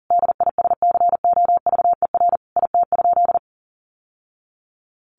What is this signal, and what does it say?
Morse code, heard as a keyed tone, about 40 wpm, 700 Hz. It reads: DISCOVER IT?
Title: Sentences from Top 1100 Words 40wpm